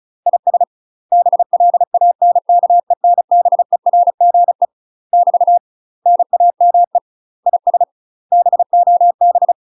Morse code, heard as a keyed tone, about 34 wpm, 700 Hz. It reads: IS BLANKENBERGE <BT> NAME IS BOB
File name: UK MASTER QSOs TRAINING / 35WPM / UK MASTER QSOs 0028